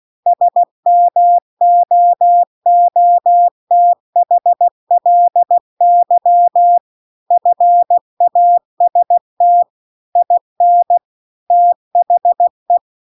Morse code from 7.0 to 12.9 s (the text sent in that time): FAST IN THE